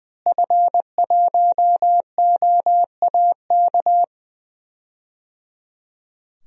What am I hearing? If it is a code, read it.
Morse code, heard as a keyed tone, about 20 wpm, 700 Hz. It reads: F1OAK
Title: DXCC-100 Call Signs - 5 Characters - Repeated 3x - 20wpm